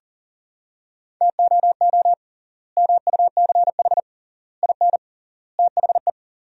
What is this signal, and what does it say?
Morse code, heard as a keyed tone, about 37 wpm, 700 Hz. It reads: TOO MUCH IN THE
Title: 3x Word-Spacing - Sentences from Top 200 Words - 40wpm